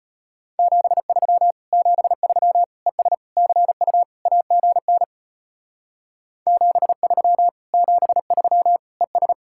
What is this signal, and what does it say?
Morse code, heard as a keyed tone, about 31 wpm, 700 Hz. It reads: 73 73 ES CU AGN 73 73 ES
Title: Speed-Racing Common QSO Phrases - 25wpm